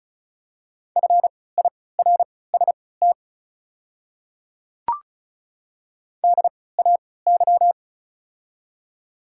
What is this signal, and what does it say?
Morse code, heard as a keyed tone, about 36 wpm, spaced as for 20 wpm, 700 Hz. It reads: FIRST DAY
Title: 3x Character Spacing - Top 100 Words - 35wpm